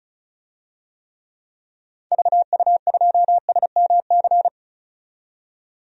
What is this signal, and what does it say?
Morse code, heard as a keyed tone, about 34 wpm, 700 Hz. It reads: VU2SMC
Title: DXCC-100 Call Signs - 6 Characters - 35wpm